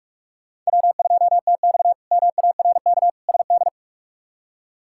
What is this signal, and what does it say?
Morse code, heard as a keyed tone, about 42 wpm, 700 Hz. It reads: W2TX MARK SD
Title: K1USN SST QSO Practice - 45wpm